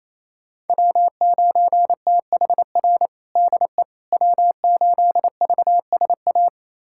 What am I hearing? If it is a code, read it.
Morse code, heard as a keyed tone, about 28 wpm, 690 Hz. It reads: W9THR DE W8VSA
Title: USA Callsign DE Callsign - 5 Characters - 28wpm